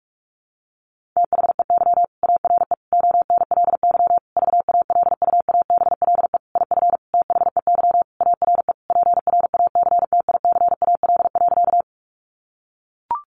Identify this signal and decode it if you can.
Morse code, heard as a keyed tone, about 44 wpm, 700 Hz. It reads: THEY ARE ONLY VALUABLE IF THEY ARE PRACTICAL.
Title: Sentences from Top 1400 Words 45wpm